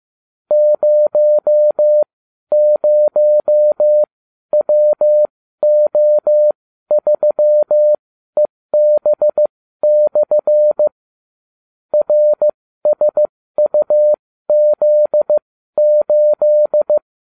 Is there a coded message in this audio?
Yes: Morse code from 0.5 to 17.0 s, heard as a keyed tone, about 15 wpm, spaced as for 12 wpm, 600 Hz. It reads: 00WO3EB/ RSUZ8